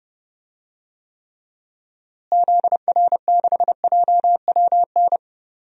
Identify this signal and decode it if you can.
Morse code, heard as a keyed tone, about 31 wpm, 700 Hz. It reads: ZR6JWN
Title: DXCC-100 Call Signs - 6 Characters - Repeated 2x - 30wpm